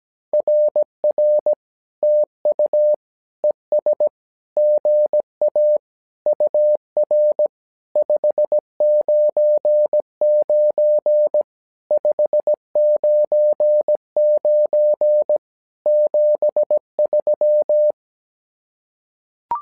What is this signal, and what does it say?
Morse code, heard as a keyed tone, about 17 wpm, 600 Hz. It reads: RR TU ES GA UR 599 599 73
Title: SOTA QSO Practice 17wpm